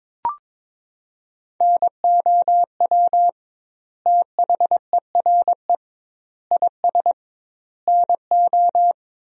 Morse code, heard as a keyed tone, about 22 wpm, 700 Hz. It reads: NOW THERE IS NO
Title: 2x Word-Spacing - Sentences from Top 100 Words - 22wpm